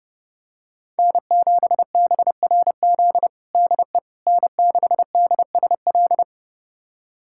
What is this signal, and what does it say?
Morse code, heard as a keyed tone, about 32 wpm, 700 Hz. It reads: N7BRZ DE N6DSL
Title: USA Callsign DE Callsign - 5 Characters - 30wpm